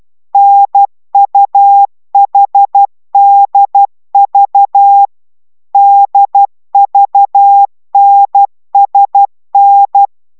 Morse code, heard as a keyed tone, about 12 wpm, 800 Hz. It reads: NUHDV DVNSN